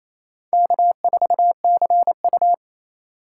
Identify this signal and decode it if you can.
Morse code, heard as a keyed tone, about 28 wpm, 700 Hz. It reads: K4CU
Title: USA Call Signs - 4 Characters - 3x Character Spacing - 28wpm